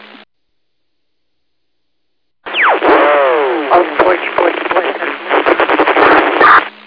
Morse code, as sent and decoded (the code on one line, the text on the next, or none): none